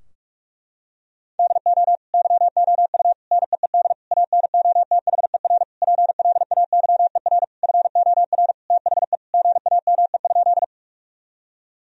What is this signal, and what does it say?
Morse code, heard as a keyed tone, about 46 wpm, 700 Hz. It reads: DO YOU NEED ANOTHER PLAYER FOR THE GAME?
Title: Sentences from Top 1000 Words - Review Part 4 - 45wpm